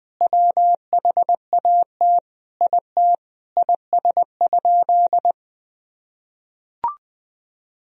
Morse code, heard as a keyed tone, about 21 wpm, 700 Hz. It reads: WHAT IT IS?
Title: Sentences from Top 100 Words and Abbreviations 20wpm